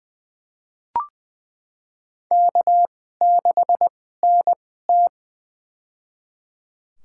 Morse code, heard as a keyed tone, about 21 wpm, spaced as for 15 wpm, 700 Hz. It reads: K6NT
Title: USA Call Signs - 4 Characters - 2x Character Spacing - 20wpm